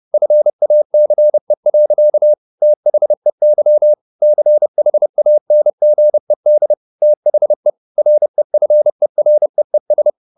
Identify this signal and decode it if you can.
Morse code, heard as a keyed tone, about 30 wpm, 600 Hz. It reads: FACE. THEY CHANGED THE REFEREES
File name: the old man and the sea 2 30wpm